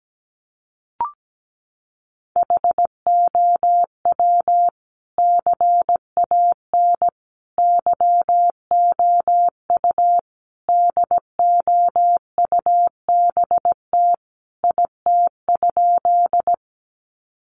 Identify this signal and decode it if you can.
Morse code, heard as a keyed tone, about 17 wpm, 700 Hz. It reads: HOW CAN YOU DOUBT IT?
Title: Sentences from Top 1500 Words 17wpm